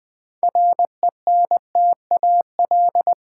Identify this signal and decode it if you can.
Morse code, heard as a keyed tone, about 20 wpm, 700 Hz. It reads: RENTAL
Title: 6 Letter Words - First 3 Letters Given - 20wpm